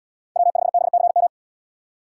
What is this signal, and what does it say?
Morse code, heard as a keyed tone, about 71 wpm, 700 Hz. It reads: W5BCM